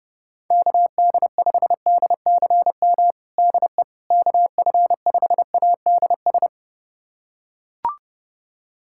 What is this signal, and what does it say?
Morse code, heard as a keyed tone, about 30 wpm, 700 Hz. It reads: KD5DCM DE KF5ADS
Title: USA Callsign DE Callsign 30wpm